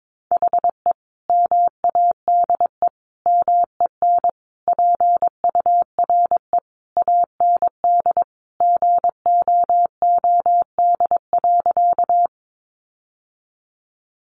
Morse code, heard as a keyed tone, about 22 wpm, 700 Hz. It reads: HE MADE MEN PURE AND GOOD.
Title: Sentences from Top 1500 Words 22wpm